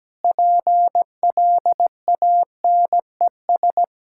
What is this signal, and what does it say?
Morse code, heard as a keyed tone, about 17 wpm, 700 Hz. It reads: PLANES
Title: Pangram Challenge v2 - 17wpm